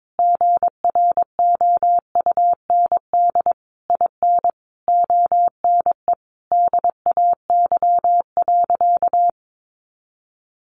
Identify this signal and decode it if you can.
Morse code, heard as a keyed tone, about 22 wpm, 700 Hz. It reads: GROUND IN ONE DAY.
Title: Sentences from Top 500 Words 22wpm